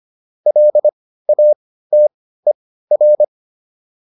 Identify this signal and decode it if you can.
Morse code, heard as a keyed tone, about 25 wpm, spaced as for 15 wpm, 600 Hz. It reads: LATER